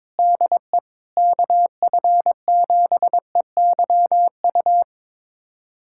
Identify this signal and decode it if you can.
Morse code, heard as a keyed tone, about 22 wpm, 700 Hz. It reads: DE KF7EYU
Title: USA Callsign DE Callsign 22wpm